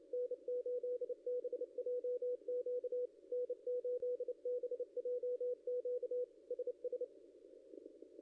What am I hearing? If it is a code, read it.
Morse code, heard as a keyed tone, about 30 wpm, 490 Hz, with radio noise. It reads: N8BJQ N8BJQ SS
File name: contests / ss-cw-k5tr-op-n5rz -0700